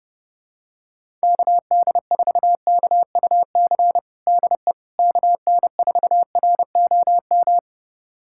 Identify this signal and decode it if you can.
Morse code, heard as a keyed tone, about 30 wpm, 700 Hz. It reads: KD4KUC DE KN4ROM
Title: USA Callsign DE Callsign 30wpm